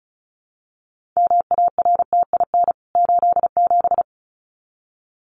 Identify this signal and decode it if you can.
Morse code, heard as a keyed tone, about 34 wpm, 700 Hz. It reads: MARTIN 87
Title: CWT Exchange 35wpm